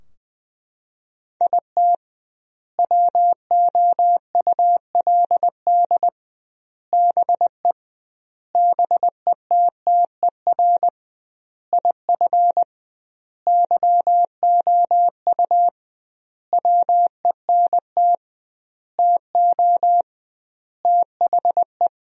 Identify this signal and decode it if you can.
Morse code, heard as a keyed tone, about 20 wpm, 700 Hz. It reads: IT WOULD BE BETTER IF YOU WENT TO THE
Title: 2x Word-Spacing - Sentences from Top 400 Words - 20wpm